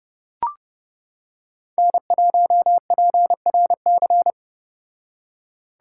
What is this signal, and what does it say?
Morse code, heard as a keyed tone, about 30 wpm, 700 Hz. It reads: N1PRC